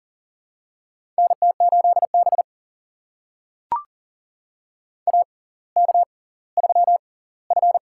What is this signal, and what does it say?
Morse code, heard as a keyed tone, about 40 wpm, spaced as for 20 wpm, 700 Hz. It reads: NT8B AK3F
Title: USA Call Signs - 6x Character Spacing - 40wpm